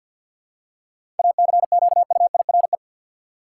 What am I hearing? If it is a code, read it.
Morse code, heard as a keyed tone, about 52 wpm, 700 Hz. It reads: ACQUIRE
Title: Seven Letter Words - 50wpm